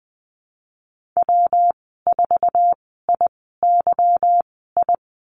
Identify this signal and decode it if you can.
Morse code, heard as a keyed tone, about 20 wpm, spaced as for 15 wpm, 700 Hz. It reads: W4IYI
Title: 167 Lesson 27g - Number 7 - Call Signs 20wpm